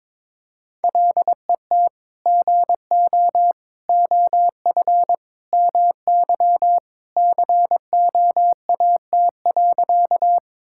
Morse code, heard as a keyed tone, about 21 wpm, 700 Hz. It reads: LET GO OF MY COAT.
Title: Sentences from Top 1600 Words 22wpm